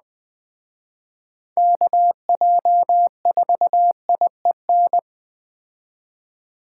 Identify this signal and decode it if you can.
Morse code, heard as a keyed tone, about 20 wpm, 700 Hz. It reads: KJ4IEN
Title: Speed Racing - USA 2x3 Call Signs 17wpm